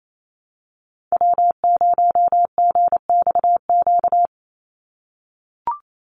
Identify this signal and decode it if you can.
Morse code, heard as a keyed tone, about 28 wpm, 700 Hz. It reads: W0GXQ
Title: Call Signs - 28wpm